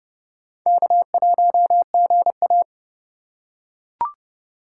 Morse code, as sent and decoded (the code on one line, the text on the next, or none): -.- .---- --. .-
K1GA